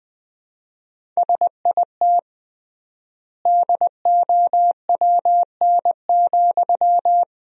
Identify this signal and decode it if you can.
Morse code, heard as a keyed tone, about 20 wpm, 700 Hz. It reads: SIT DOWN,
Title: 3x Word-Spacing - Sentences from Top 500 Words - 20wpm